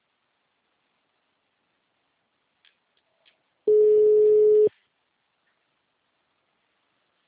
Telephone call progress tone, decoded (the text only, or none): ringback tone